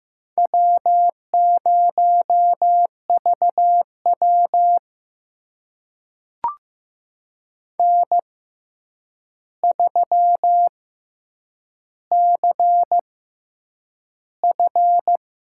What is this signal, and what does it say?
Morse code, heard as a keyed tone, about 15 wpm, spaced as for 8 wpm, 700 Hz. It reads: W0VW N3CF
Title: USA Call Signs - 4 Characters - 6x Character Spacing - 15wpm